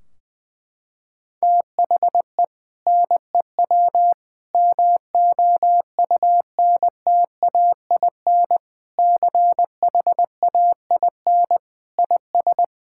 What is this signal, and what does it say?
Morse code, heard as a keyed tone, about 20 wpm, 700 Hz. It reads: THE NEW MOUNTAIN CHAIN IS
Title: Sentences from Top 1600 Words 20wpm